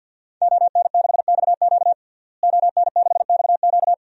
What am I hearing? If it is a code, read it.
Morse code, heard as a keyed tone, about 53 wpm, 700 Hz. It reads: ON6XQ ON6XQ
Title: DXCC-100 Call Signs - 5 Characters - Repeated 2x - 50wpm